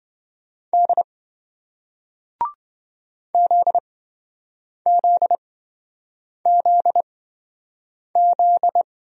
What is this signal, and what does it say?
Morse code, heard as a keyed tone, about 27 wpm, spaced as for 7 wpm, 700 Hz. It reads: D ZZZZ